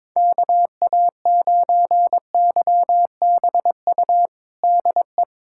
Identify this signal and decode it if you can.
Morse code, heard as a keyed tone, about 22 wpm, 700 Hz. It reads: KA9YBU DE
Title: USA Callsign DE Callsign 22wpm